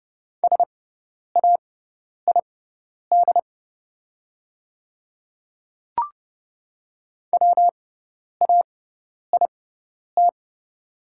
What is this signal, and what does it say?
Morse code, heard as a keyed tone, about 30 wpm, spaced as for 10 wpm, 700 Hz. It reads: SAID WAIT
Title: 6x Character Spacing - Top 500 Words - 30wpm